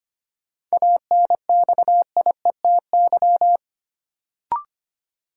Morse code, heard as a keyed tone, about 25 wpm, 700 Hz. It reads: ANXIETY
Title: Sentences from Top 1600 Words - New Word Drill Alt - 25wpm